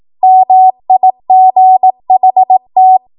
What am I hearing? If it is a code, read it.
Morse code, heard as a keyed tone, about 18 wpm, 750 Hz. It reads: MIGHT
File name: Morse / 18WPM